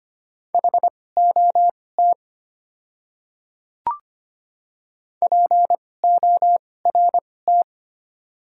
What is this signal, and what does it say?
Morse code, heard as a keyed tone, about 25 wpm, spaced as for 18 wpm, 700 Hz. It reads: HOT PORT
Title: 2x Character Spacing - Top 200 Words - 25wpm